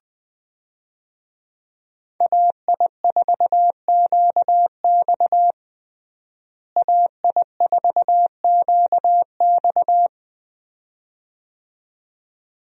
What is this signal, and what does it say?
Morse code, heard as a keyed tone, about 20 wpm, 700 Hz. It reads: AI4QX AI4QX
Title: USA Call Signs - 5 Characters - Repeated 2x - 20wpm